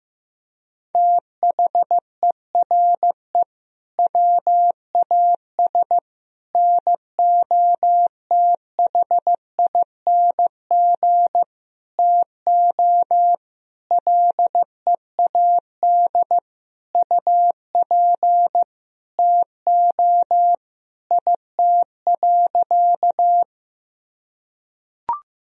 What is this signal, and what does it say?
Morse code, heard as a keyed tone, about 15 wpm, 700 Hz. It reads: THERE WAS NOTHING TO LEAD UP TO IT.
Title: Sentences from Top 1000 Words - Review Part 2 - 15wpm